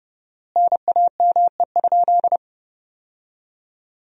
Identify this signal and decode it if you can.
Morse code, heard as a keyed tone, about 32 wpm, 700 Hz. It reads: NAME?
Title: 4x Word-Spacing - Sentences from Top 200 Words - 30wpm